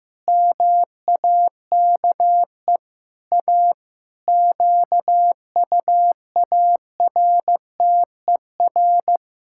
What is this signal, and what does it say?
Morse code, heard as a keyed tone, about 16 wpm, 700 Hz. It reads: MAKE A QUARTER
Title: Pangram Challenge v2 - 15wpm